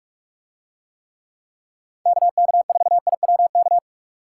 Morse code, heard as a keyed tone, about 44 wpm, 700 Hz. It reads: KK4IWK